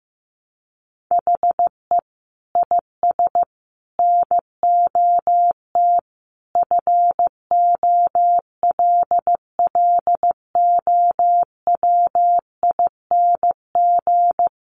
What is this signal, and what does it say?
Morse code, heard as a keyed tone, about 15 wpm, 700 Hz. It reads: HE IS NOT FOLLOWING
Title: Sentences from Top 700 Words 15wpm